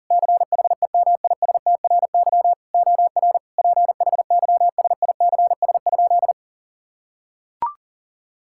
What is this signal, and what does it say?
Morse code, heard as a keyed tone, about 37 wpm, 700 Hz. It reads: CHEMISTRY OR PHYSICS?
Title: Sentences from Top 1000 Words - Review Part 6 - 40wpm